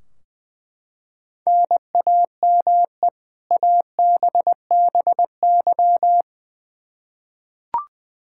Morse code, heard as a keyed tone, about 20 wpm, 700 Hz. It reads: NAME ABBY
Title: Practice QSO - Exchange 1 - Beginning - 20wpm